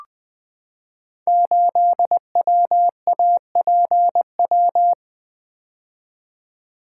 Morse code, heard as a keyed tone, about 20 wpm, 700 Hz. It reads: 8WAPW
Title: Code Groups - 5 Letter:Numbers - 20wpm